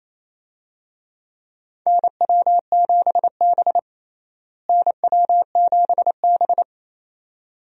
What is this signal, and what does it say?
Morse code, heard as a keyed tone, about 28 wpm, 700 Hz. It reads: NW7B NW7B
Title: USA Call Signs - 4 Characters - Repeated 2x - 28wpm